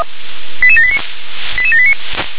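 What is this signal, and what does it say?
Five-tone selective call (ZVEI), heard as quick runs of five tones, tones of about 70 ms. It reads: 88789 88789
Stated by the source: ZVEI five-tone selective call received on SDR.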